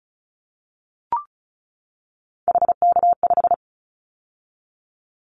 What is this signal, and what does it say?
Morse code, heard as a keyed tone, about 36 wpm, 700 Hz. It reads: HK5
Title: Callsign Prefixes for Top 100 DXCC - 35wpm